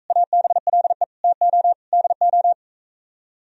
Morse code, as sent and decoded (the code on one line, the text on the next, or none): .- -... .-.. / - --- / -.. ---
ABLE TO DO